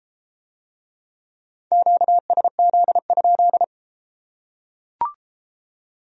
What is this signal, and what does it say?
Morse code, heard as a keyed tone, about 33 wpm, 700 Hz. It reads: QSZ?